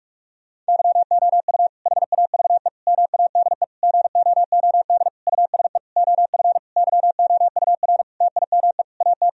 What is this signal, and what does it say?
Morse code, heard as a keyed tone, about 41 wpm, 700 Hz. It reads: YOU HAVE MADE GOOD USE OF YOUR TIME AT
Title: Sentences from Top 1000 Words - Review Part 1 - 45wpm